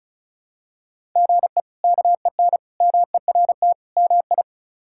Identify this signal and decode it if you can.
Morse code, heard as a keyed tone, about 36 wpm, 700 Hz. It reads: GE KEN MERT MI
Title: K1USN SST QSO Practice - 35wpm